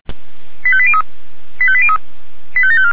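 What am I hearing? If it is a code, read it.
Five-tone selective call (ZVEI), heard as quick runs of five tones, tones of about 70 ms. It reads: 85803 85803 85754
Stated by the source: ZVEI five-tone call received with an SDR, logged by the Feuerwehr Orlamünde audio server.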